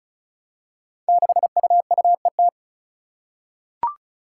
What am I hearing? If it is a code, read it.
Morse code, heard as a keyed tone, about 36 wpm, 700 Hz. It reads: 6UUET
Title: Five Challenging Letter-Number Combinations 35wpm